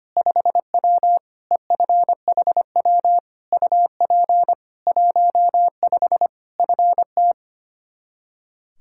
Morse code, heard as a keyed tone, about 25 wpm, 700 Hz. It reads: 5W EFHW UP 15 FT
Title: K1USN SST QSO Practice 25wpm